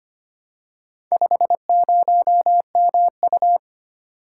Morse code, heard as a keyed tone, about 25 wpm, 700 Hz. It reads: 50MU
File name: practice / 25wpm / Four Letter-Number Combinations - 25wpm